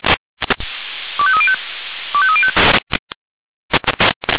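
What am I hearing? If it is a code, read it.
Five-tone selective call (ZVEI), heard as quick runs of five tones, tones of about 70 ms. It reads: 25305 25305
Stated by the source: ZVEI five-tone selective call received on SDR.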